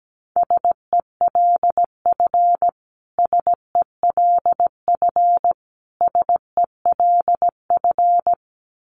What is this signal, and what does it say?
Morse code, heard as a keyed tone, about 17 wpm, 700 Hz. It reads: SELF SELF SELF
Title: Top 500 Words Repeated 17wpm